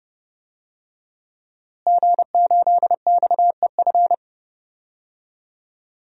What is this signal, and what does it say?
Morse code, heard as a keyed tone, about 30 wpm, 700 Hz. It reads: G8XEF
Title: DXCC-100 Call Signs - 5 Characters - Repeated 2x - 30wpm